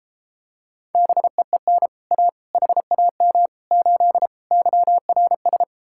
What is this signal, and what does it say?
Morse code, heard as a keyed tone, about 33 wpm, 700 Hz. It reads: BEEN A HAM 8 YRS